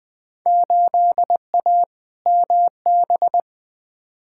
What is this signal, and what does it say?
Morse code, heard as a keyed tone, about 20 wpm, 700 Hz. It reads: 8A MB
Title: Field Day Exchanges Encore - 20wpm